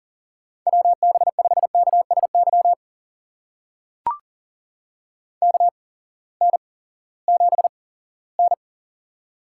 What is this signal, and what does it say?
Morse code, heard as a keyed tone, about 40 wpm, spaced as for 20 wpm, 700 Hz. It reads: WB5KSY KN7N
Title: USA Call Signs - 8x Character Spacing - 40wpm